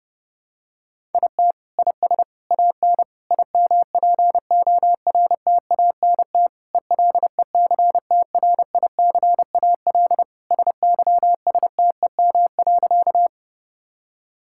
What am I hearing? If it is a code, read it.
Morse code, heard as a keyed tone, about 30 wpm, 700 Hz. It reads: IT IS AN IMPORTANT ELECTRICAL SYSTEM.
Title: Sentences from Top 1000 Words - Review Part 6 - 30wpm